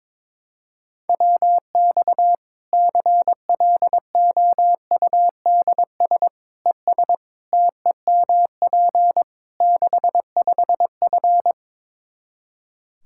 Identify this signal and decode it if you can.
Morse code, heard as a keyed tone, about 22 wpm, 700 Hz. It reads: WX CLOUDS ES TEMP 65F
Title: Practice QSO - Exchange 2 - Info - 22wpm